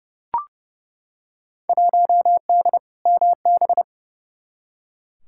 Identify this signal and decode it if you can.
Morse code, heard as a keyed tone, about 30 wpm, 700 Hz. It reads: 1D MB
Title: Field Day Exchanges Encore - 30wpm